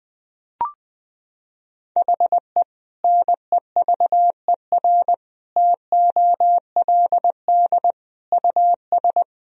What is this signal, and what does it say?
Morse code, heard as a keyed tone, about 20 wpm, 700 Hz. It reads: HE NEVER TOLD US
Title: Sentences from Top 400 Words and Abbreviations 20wpm